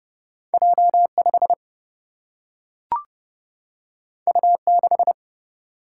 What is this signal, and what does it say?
Morse code, heard as a keyed tone, about 31 wpm, 700 Hz. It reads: J5 U6
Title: Single-Letter Single-Number 30wpm